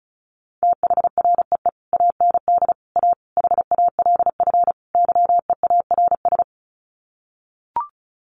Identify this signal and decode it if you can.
Morse code, heard as a keyed tone, about 34 wpm, 700 Hz. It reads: THREE AND A HALF YEARS
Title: Sets of 5 Words Encore2 35wpm